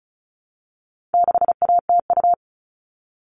Morse code, heard as a keyed tone, about 34 wpm, 700 Hz. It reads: 6ATU
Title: All Cut Numbers - 4-Digit 35wpm